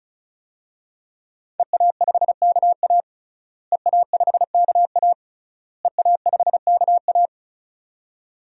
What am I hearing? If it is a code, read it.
Morse code, heard as a keyed tone, about 36 wpm, 700 Hz. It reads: EA5KA EA5KA EA5KA
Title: DXCC-100 Call Signs - 5 Characters - Repeated 3x - 35wpm